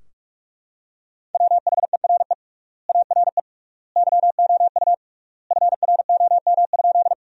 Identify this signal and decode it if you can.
Morse code, heard as a keyed tone, about 41 wpm, 700 Hz. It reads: WHERE ARE YOU FROM?
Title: 3x Word-Spacing - Sentences from Top 200 Words - 45wpm